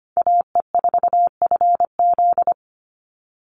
Morse code, heard as a keyed tone, about 25 wpm, 700 Hz. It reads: AE4FZ
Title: NA SOTA Chaser Callsigns 25wpm